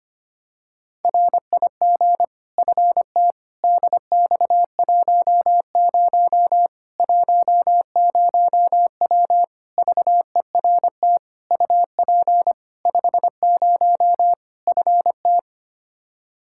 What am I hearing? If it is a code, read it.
Morse code, heard as a keyed tone, about 25 wpm, 700 Hz. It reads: RIG FT DX10 10W VERT UP 50 FT
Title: Practice QSO - Exchange 2 - Info - 25wpm